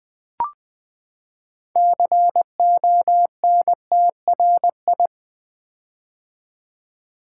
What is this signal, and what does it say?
Morse code, heard as a keyed tone, about 20 wpm, 700 Hz. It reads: CONTRI
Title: Build A Word - 6 Letters At A Time - 12-Letter Words - 20wpm